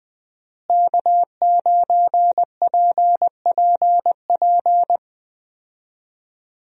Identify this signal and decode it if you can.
Morse code, heard as a keyed tone, about 21 wpm, 700 Hz. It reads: K9PPP
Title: USA Call Signs - 5 Characters - Repeated 3x - 20wpm